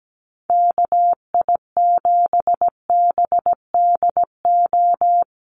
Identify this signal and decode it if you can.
Morse code, heard as a keyed tone, about 17 wpm, 700 Hz. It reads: KI7BDO
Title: USA Call Signs Repeated 2x 17wpm